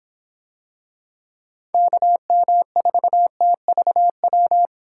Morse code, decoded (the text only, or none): KM4TVW